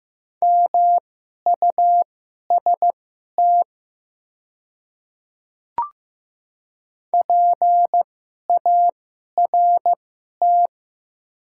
Morse code, heard as a keyed tone, about 15 wpm, spaced as for 11 wpm, 700 Hz. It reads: MUST PART
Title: 2x Character Spacing - Top 200 Words - 15wpm